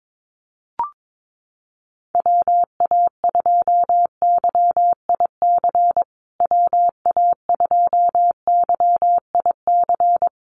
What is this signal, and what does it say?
Morse code, heard as a keyed tone, about 22 wpm, 700 Hz. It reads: WA2YIC WA2YIC
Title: USA Call Signs Repeated 3x 22wpm